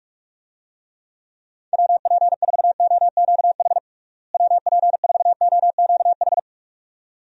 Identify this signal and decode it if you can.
Morse code, heard as a keyed tone, about 46 wpm, 700 Hz. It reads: WP4OQH WP4OQH
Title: USA Call Signs - 6 Characters - Repeated 2x - 45wpm